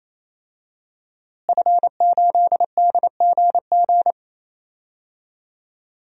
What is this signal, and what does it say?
Morse code, heard as a keyed tone, about 28 wpm, 700 Hz. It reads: F8DGG
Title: DXCC-100 Call Signs - 5 Characters - Repeated 3x - 28wpm